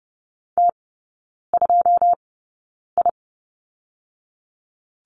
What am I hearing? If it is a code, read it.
Morse code, heard as a keyed tone, about 30 wpm, spaced as for 9 wpm, 700 Hz. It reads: T2I
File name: Lesson 18e - Letter m - Three Characters 30wpm